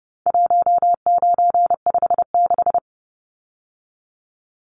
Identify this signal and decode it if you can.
Morse code, heard as a keyed tone, about 30 wpm, 700 Hz. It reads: 1956
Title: Numbers - 4-Digit 30wpm